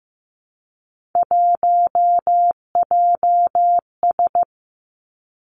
Morse code, heard as a keyed tone, about 15 wpm, 700 Hz. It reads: 1JS
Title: Three Challenging Letter-Number Combinations 15wpm